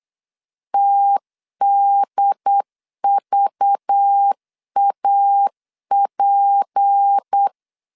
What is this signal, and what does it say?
Morse code, heard as a keyed tone, about 8 wpm, 800 Hz. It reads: TDVAP
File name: CW / DET / 1 6 1500C S B 8wpm 5142